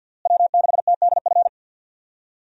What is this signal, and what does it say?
Morse code, heard as a keyed tone, about 50 wpm, 700 Hz. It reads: W6TDF
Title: USA Call Signs - 5 Characters - Repeated 3x - 50wpm